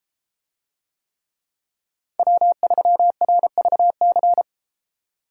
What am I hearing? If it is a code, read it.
Morse code, heard as a keyed tone, about 32 wpm, 700 Hz. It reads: W3RVC